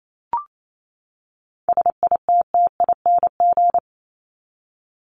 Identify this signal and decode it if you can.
Morse code, heard as a keyed tone, about 28 wpm, 700 Hz. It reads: SITTING
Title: ING Suffix - Top 100 Words with Base Words 28wpm